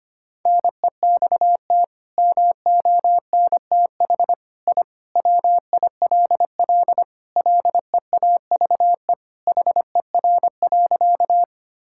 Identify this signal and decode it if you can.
Morse code, heard as a keyed tone, about 25 wpm, 700 Hz. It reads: NEXT MONTH I WILL LEAVE HER.
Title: Sentences from Top 1000 Words - Review Part 5 - 25wpm